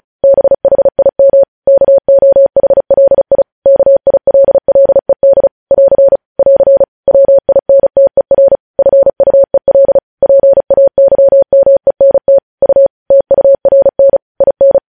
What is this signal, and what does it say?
Morse code, heard as a keyed tone, about 36 wpm, 550 Hz. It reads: BHIM KOHLI KILLED <AR> <AR> WINTER FUEL PAYMENT U TURN IN
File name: podcast / News headlines-1x-35x35